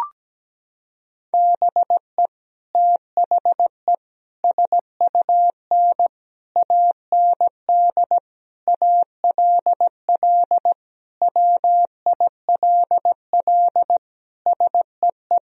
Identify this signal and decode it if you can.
Morse code, heard as a keyed tone, about 17 wpm, 700 Hz. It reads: BE THE SUN AND ALL WILL SEE
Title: Sentences from Top 1000 Words - Review Part 1 - 17wpm